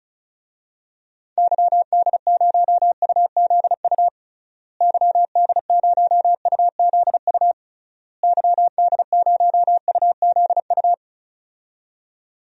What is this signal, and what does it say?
Morse code, heard as a keyed tone, about 36 wpm, 700 Hz. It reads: YD0UZU YD0UZU YD0UZU
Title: DXCC-100 Call Signs - 6 Characters - Repeated 3x - 35wpm